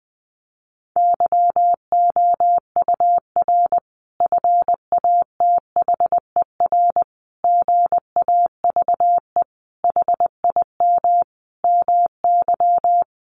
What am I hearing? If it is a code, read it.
Morse code, heard as a keyed tone, about 20 wpm, 700 Hz. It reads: YOUR FATHER GAVE HIM MY
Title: Sentences from Top 1400 Words 20wpm